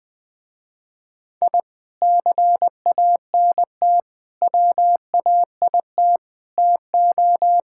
Morse code, heard as a keyed tone, about 21 wpm, 700 Hz. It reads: I CANT WAIT TO
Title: Sentences with 3-Word Phrases - Spoken Beginning - 20wpm